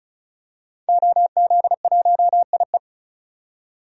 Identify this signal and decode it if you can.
Morse code, heard as a keyed tone, about 36 wpm, 700 Hz. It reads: OZ1IE
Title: DXCC-100 Call Signs - 5 Characters - 35wpm